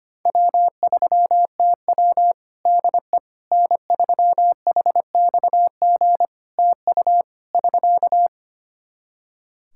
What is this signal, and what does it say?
Morse code, heard as a keyed tone, about 25 wpm, 700 Hz. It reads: W3TW DE N3HXG TU <SK>
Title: Practice QSO - Exchange 4 - Ending - 25wpm